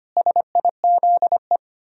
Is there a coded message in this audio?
Yes: Morse code from 0.2 to 1.6 s, heard as a keyed tone, about 33 wpm, 700 Hz. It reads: SIZE